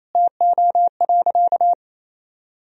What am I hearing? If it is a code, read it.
Morse code, heard as a keyed tone, about 28 wpm, 700 Hz. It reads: TO.
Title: Sentences from Top 1000 Words - Review Part 3 - 28wpm